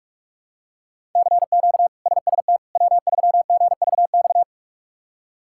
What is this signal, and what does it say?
Morse code, heard as a keyed tone, about 46 wpm, 700 Hz. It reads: CQ SST W3GVX
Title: K1USN SST QSO Practice - 45wpm